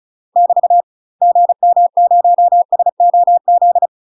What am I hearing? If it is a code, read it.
Morse code, heard as a keyed tone, about 34 wpm, 700 Hz. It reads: <BT> GM0SOZ
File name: UK MASTER QSOs TRAINING / 35WPM / UK MASTER QSOs 0043